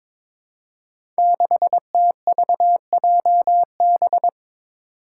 Morse code, heard as a keyed tone, about 22 wpm, 700 Hz. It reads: 6TVJB
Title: Five Challenging Letter-Number Combinations 22wpm